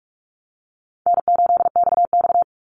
Morse code, heard as a keyed tone, about 44 wpm, 700 Hz. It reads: N8XX
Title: Call Signs - 45wpm